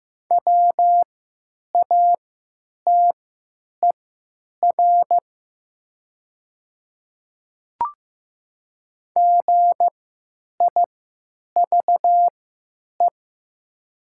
Morse code, heard as a keyed tone, about 15 wpm, spaced as for 9 wpm, 700 Hz. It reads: WATER GIVE
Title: 3x Character Spacing - Top 200 Words - 15wpm